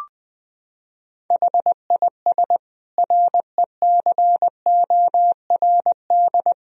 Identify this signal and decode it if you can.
Morse code, heard as a keyed tone, about 20 wpm, 700 Hz. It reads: HIS RECORD
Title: Sentences from Top 1000 Words - Review Part 3 - 20wpm